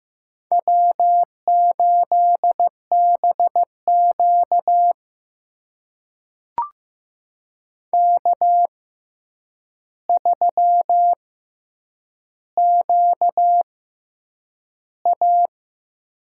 Morse code, heard as a keyed tone, about 15 wpm, spaced as for 8 wpm, 700 Hz. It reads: W8BQ K3QA
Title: USA Call Signs - 4 Characters - 6x Character Spacing - 15wpm